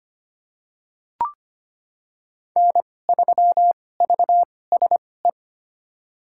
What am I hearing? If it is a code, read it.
Morse code, heard as a keyed tone, about 25 wpm, spaced as for 18 wpm, 700 Hz. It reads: N3VSE
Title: USA Call Signs - 5 Characters - 2x Character Spacing - 25wpm